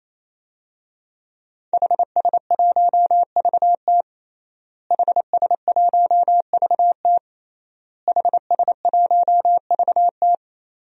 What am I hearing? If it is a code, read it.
Morse code, heard as a keyed tone, about 28 wpm, 700 Hz. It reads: HS1VT HS1VT HS1VT
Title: DXCC-100 Call Signs - 5 Characters - Repeated 3x - 28wpm